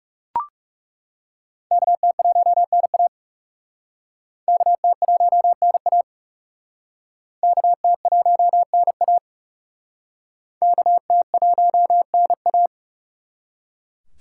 Morse code, heard as a keyed tone, about 37 wpm, 700 Hz. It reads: KT1NA KT1NA KT1NA KT1NA
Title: Speed Racing - USA 2x2 Call Signs 30wpm